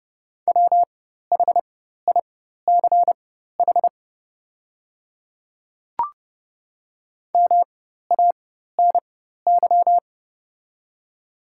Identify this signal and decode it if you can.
Morse code, heard as a keyed tone, about 30 wpm, spaced as for 14 wpm, 700 Hz. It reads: WHICH MANY
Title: 4x Character Spacing - Top 100 Words - 30wpm